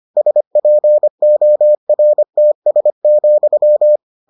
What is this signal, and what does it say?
Morse code, heard as a keyed tone, about 25 wpm, 600 Hz. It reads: SPORTS,